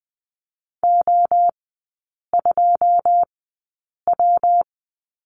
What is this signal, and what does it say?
Morse code, heard as a keyed tone, about 20 wpm, spaced as for 8 wpm, 700 Hz. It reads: O2W
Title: 121 Lesson 20e - Number 3 - Three Characters 20wpm